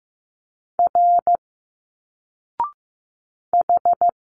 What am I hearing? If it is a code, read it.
Morse code, heard as a keyed tone, about 15 wpm, 700 Hz. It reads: R H
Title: Single Letters 15wpm